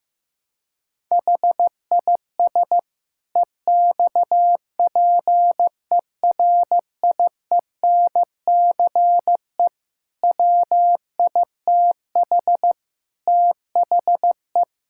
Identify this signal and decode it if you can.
Morse code, heard as a keyed tone, about 16 wpm, 700 Hz. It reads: HIS EXPERIENCE WITH THE
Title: Sentences from Top 1000 Words - Review Part 6 - 15wpm